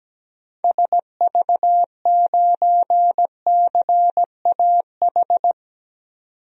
Morse code, heard as a keyed tone, about 17 wpm, 700 Hz. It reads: SV9CAH
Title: DXCC-100 Call Signs - 6 Characters - 17wpm